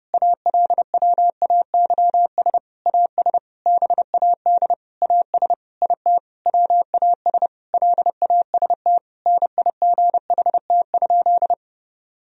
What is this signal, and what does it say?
Morse code, heard as a keyed tone, about 32 wpm, 700 Hz. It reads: ALWAYS AS BAD AS IT WAS LAST NIGHT?
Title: Sentences from Top 1000 Words - Review Part 1 - 30wpm